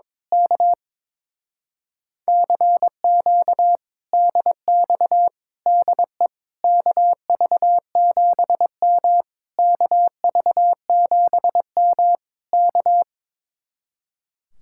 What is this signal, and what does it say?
Morse code, heard as a keyed tone, about 22 wpm, 700 Hz. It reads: K CQ DX DE KV7M KV7M K
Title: Speed-Racing Common QSO Phrases - 22wpm